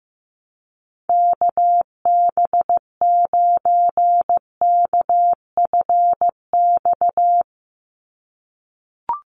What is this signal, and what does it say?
Morse code, heard as a keyed tone, about 15 wpm, 700 Hz. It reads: KB9KFX
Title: USA Call Signs 15wpm